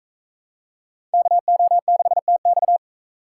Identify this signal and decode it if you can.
Morse code, heard as a keyed tone, about 43 wpm, 700 Hz. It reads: KO6TX